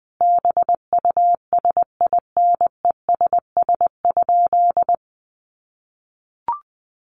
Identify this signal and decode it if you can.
Morse code, heard as a keyed tone, about 20 wpm, 700 Hz. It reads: BUSINESS?
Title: Sentences from Top 1100 Words 20wpm